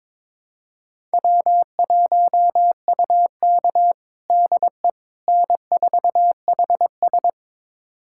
Morse code, heard as a keyed tone, about 22 wpm, 700 Hz. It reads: W1UK DE N4HS
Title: USA Callsign DE Callsign - 4 Characters - 22wpm